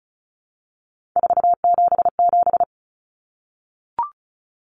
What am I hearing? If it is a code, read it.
Morse code, heard as a keyed tone, about 34 wpm, 700 Hz. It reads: 477
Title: Common Cut Numbers - 3-Digit 35wpm